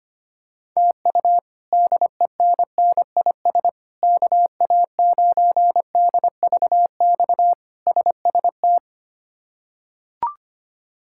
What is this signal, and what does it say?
Morse code, heard as a keyed tone, about 25 wpm, 700 Hz. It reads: TU DENNIS KA9DVX SST
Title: K1USN SST QSO Practice - 25wpm